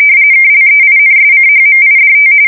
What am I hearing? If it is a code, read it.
RTTY, about 45.45 baud. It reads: KLEINSCHMIDT
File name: fsk170 / teletype-corp